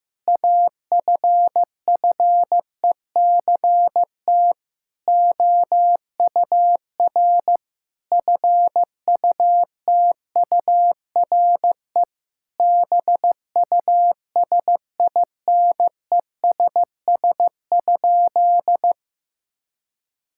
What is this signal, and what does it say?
Morse code, heard as a keyed tone, about 15 wpm, 700 Hz. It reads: AFFECT OUR FUTURE BUSINESS?
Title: Sentences from Top 1000 Words - Review Part 6 - 15wpm